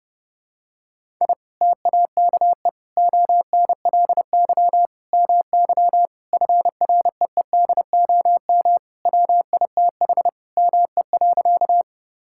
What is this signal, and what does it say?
Morse code, heard as a keyed tone, about 30 wpm, 700 Hz. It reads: I TAKE ONLY MY FREEDOM WITH ME.
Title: Sentences from Top 1000 Words - Review Part 5 - 30wpm